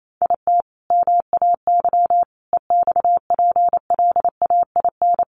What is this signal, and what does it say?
Morse code, heard as a keyed tone, about 28 wpm, 700 Hz. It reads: IT MAY EXPLAIN
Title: Sentences from Top 1300 Words 28wpm